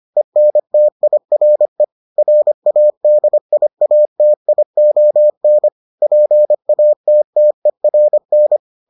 Morse code, heard as a keyed tone, about 25 wpm, 600 Hz. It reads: ENTIRE RADIATION PATTERN